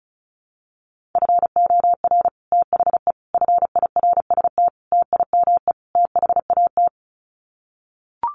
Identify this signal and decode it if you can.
Morse code, heard as a keyed tone, about 34 wpm, 700 Hz. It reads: FOR THE FIRST TIME THAT
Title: Sets of 5 Words Encore 35wpm